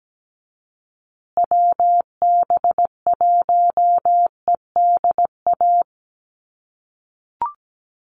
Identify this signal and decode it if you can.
Morse code, heard as a keyed tone, about 17 wpm, 700 Hz. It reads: WB1EDA
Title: USA Call Signs Repeated 2x 17wpm